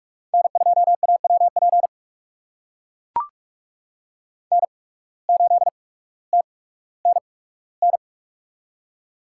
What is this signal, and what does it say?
Morse code, heard as a keyed tone, about 44 wpm, spaced as for 19 wpm, 700 Hz. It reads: N2AWP N8TNN
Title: USA Call Signs - 5 Characters - 8x Character Spacing - 45wpm